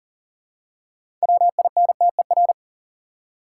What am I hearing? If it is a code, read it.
Morse code, heard as a keyed tone, about 37 wpm, 700 Hz. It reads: WINTER
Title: Six Letter Words - 40wpm